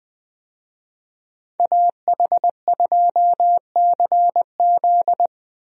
Morse code, heard as a keyed tone, about 20 wpm, 700 Hz. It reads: AH2CZ
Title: USA Call Signs - 5 Characters - Repeated 3x - 20wpm